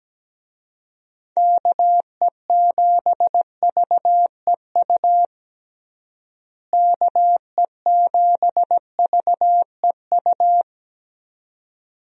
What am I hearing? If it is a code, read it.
Morse code, heard as a keyed tone, about 17 wpm, 700 Hz. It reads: KE7VEU KE7VEU
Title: USA Call Signs - 6 Characters - Repeated 2x - 17wpm